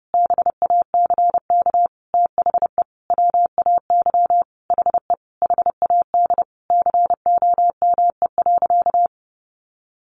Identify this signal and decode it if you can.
Morse code, heard as a keyed tone, about 30 wpm, 700 Hz. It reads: BACK THE WAY HE HAD COME.
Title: Sentences from Top 1100 Words 30wpm